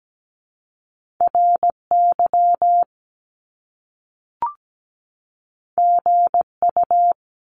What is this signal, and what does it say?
Morse code, heard as a keyed tone, about 17 wpm, 700 Hz. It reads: RY GU
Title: Top Two Letter Combinations in Words 17wpm